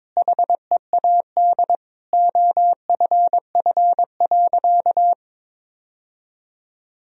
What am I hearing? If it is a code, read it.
Morse code, heard as a keyed tone, about 21 wpm, 700 Hz. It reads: HEAD OFF.
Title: Sentences from Top 1000 Words - Review Part 1 - 22wpm